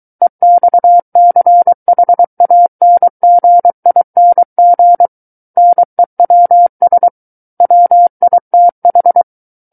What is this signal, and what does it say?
Morse code, heard as a keyed tone, about 23 wpm, 700 Hz. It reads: EXCHANGING NEWS WITH
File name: Book One / 14-Chapter 14 - 23 WPM